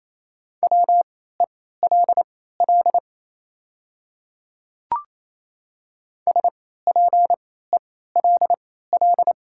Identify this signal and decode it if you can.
Morse code, heard as a keyed tone, about 28 wpm, spaced as for 16 wpm, 700 Hz. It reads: WELL SPELL
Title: 3x Character Spacing - Top 200 Words - 28wpm